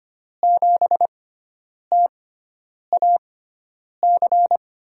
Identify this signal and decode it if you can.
Morse code, heard as a keyed tone, about 25 wpm, spaced as for 9 wpm, 700 Hz. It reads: 7TAC
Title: USA Call Signs - 6 Characters - 6x Character Spacing - 25wpm